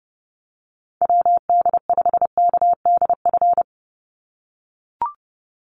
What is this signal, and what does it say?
Morse code, heard as a keyed tone, about 30 wpm, 700 Hz. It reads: WD5KDF
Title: USA Call Signs 30wpm